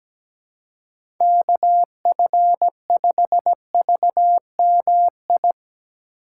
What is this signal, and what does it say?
Morse code, heard as a keyed tone, about 17 wpm, 700 Hz. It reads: KF5VMI
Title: USA Call Signs - 6 Characters - 2x Character Spacing - 17wpm